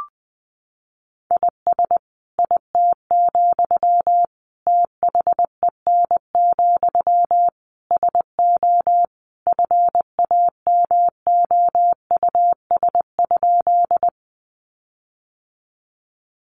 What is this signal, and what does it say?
Morse code, heard as a keyed tone, about 20 wpm, 700 Hz. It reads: IS IT, THEN, SO FAMOUS?
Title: Sentences from Top 1500 Words 20wpm